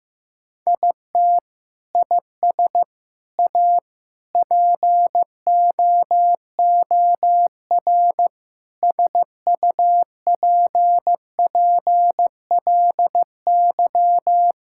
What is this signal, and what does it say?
Morse code, heard as a keyed tone, about 15 wpm, 700 Hz. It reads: IT IS A POOR SUPPLY
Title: Sentences from Top 1000 Words - Review Part 5 - 15wpm